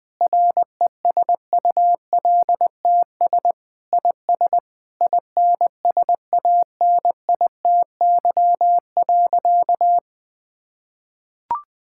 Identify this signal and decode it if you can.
Morse code, heard as a keyed tone, about 20 wpm, 700 Hz. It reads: RESULTS IS INSANITY.
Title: Sentences with 9-Word Phrases - 20wpm